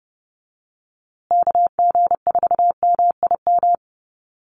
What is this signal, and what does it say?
Morse code, heard as a keyed tone, about 30 wpm, 700 Hz. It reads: KG4MIM
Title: USA Call Signs Repeated 3x 30wpm